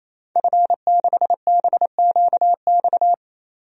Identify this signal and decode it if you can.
Morse code, heard as a keyed tone, about 28 wpm, 700 Hz. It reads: F6BQX
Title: DXCC-100 Call Signs - 5 Characters - 28wpm